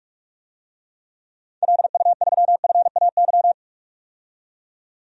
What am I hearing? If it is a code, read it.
Morse code, heard as a keyed tone, about 46 wpm, 700 Hz. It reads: LU3FAY
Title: DXCC-100 Call Signs - 6 Characters - 45wpm